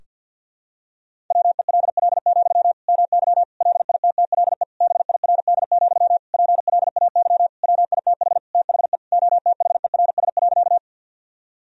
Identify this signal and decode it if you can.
Morse code, heard as a keyed tone, about 52 wpm, 700 Hz. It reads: WELL, MY LITTLE BIRD, PLAY WITH THE OTHERS.
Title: Sentences from Top 1000 Words - Review Part 2 - 50wpm